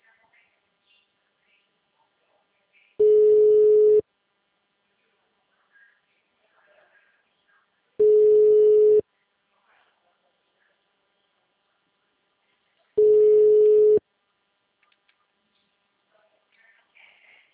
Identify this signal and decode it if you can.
Ringback tone (Europe, 425 Hz), with speech.